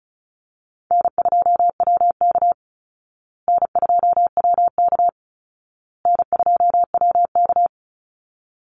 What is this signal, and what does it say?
Morse code, heard as a keyed tone, about 34 wpm, 700 Hz. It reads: N2WK N2WK N2WK
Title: Easy Call Signs Repeated 3x - 35wpm